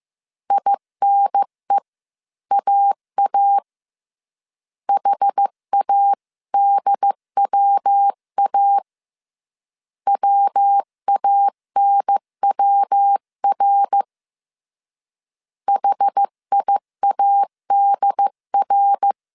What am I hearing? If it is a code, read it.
Morse code, heard as a keyed tone, about 15 wpm, 800 Hz. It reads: INEAA HADWA WANWR HIADR
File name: CW / DET / 1 3 1500C S C 14wpm 2458